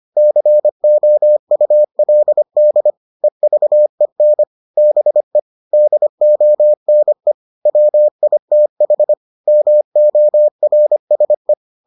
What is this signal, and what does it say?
Morse code, heard as a keyed tone, about 25 wpm, 600 Hz. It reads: COULD EVEN BE DONE WITH MORSE